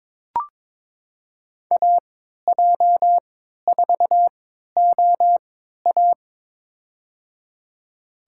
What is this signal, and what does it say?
Morse code, heard as a keyed tone, about 21 wpm, spaced as for 12 wpm, 700 Hz. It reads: AJ4OA
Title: USA Call Signs - 5 Characters - 3x Character Spacing - 22wpm